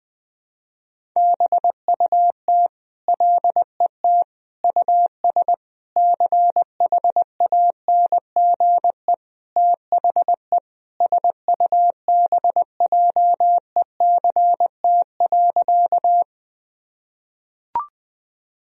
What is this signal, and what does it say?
Morse code, heard as a keyed tone, about 21 wpm, 700 Hz. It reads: BUT LET US CHANGE THE SUBJECT.
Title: Sentences from Top 1000 Words - Review Part 1 - 20wpm